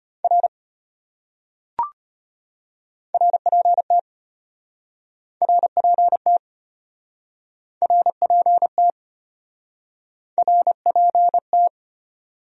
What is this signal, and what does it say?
Morse code, heard as a keyed tone, about 34 wpm, 700 Hz. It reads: R RPT RPT RPT RPT